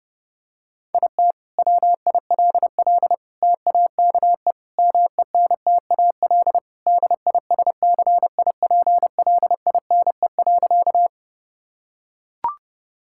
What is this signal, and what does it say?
Morse code, heard as a keyed tone, about 30 wpm, 700 Hz. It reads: IT WILL TAKE MENTAL DISCIPLINE.
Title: Sentences from Top 1000 Words - Review Part 6 - 30wpm